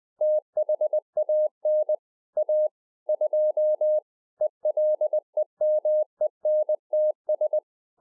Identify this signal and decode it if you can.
Morse code, heard as a keyed tone, about 20 wpm, 610 Hz. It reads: THAN A 2 ELEMENTS